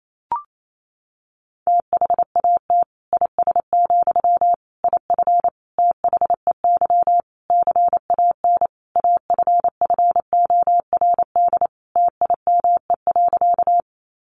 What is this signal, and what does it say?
Morse code, heard as a keyed tone, about 28 wpm, 700 Hz. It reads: THAT IS, IF THEY CAN AFFORD TIME.
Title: Sentences from Top 1100 Words 28wpm